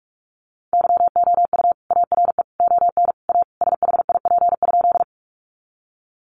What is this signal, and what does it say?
Morse code, heard as a keyed tone, about 44 wpm, 700 Hz. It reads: YOU ARE ON A SHIP?
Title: Sentences from Top 400 Words 45wpm